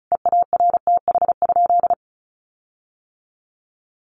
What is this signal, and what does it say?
Morse code, heard as a keyed tone, about 34 wpm, 700 Hz. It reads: EARTH?
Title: Sentences from Top 1500 Words 35wpm